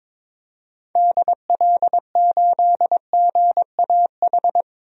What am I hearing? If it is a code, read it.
Morse code, heard as a keyed tone, about 22 wpm, 700 Hz. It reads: DL8GAH
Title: DXCC-100 Call Signs - 6 Characters - 22wpm